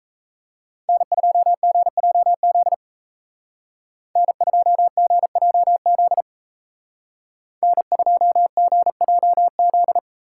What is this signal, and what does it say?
Morse code, heard as a keyed tone, about 36 wpm, 700 Hz. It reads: N2GJZ N2GJZ N2GJZ